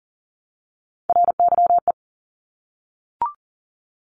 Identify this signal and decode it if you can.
Morse code, heard as a keyed tone, about 40 wpm, 700 Hz. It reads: RYE
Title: Multi-letter Phonemes Repeated 3x 40wpm